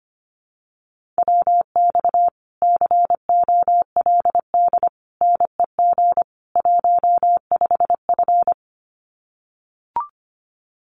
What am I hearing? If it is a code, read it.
Morse code, heard as a keyed tone, about 25 wpm, 700 Hz. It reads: WX COLD NEG 15F
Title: Common QSO Phrases Encore 25wpm